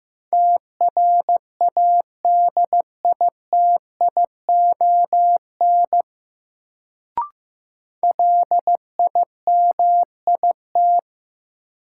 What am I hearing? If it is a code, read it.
Morse code, heard as a keyed tone, about 15 wpm, 700 Hz. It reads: TRADITION LIMIT
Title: Sentences from Top 1500 Words - New Word Drill Alt - 15wpm